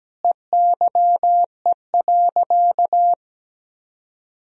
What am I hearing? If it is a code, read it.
Morse code, heard as a keyed tone, about 17 wpm, 700 Hz. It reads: EYE.
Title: Sentences from Top 1000 Words - Review Part 1 - 17wpm